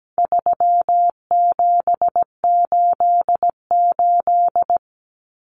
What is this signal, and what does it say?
Morse code, heard as a keyed tone, about 17 wpm, 700 Hz. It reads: 3788
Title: Common Cut Numbers - 4-Digit 17wpm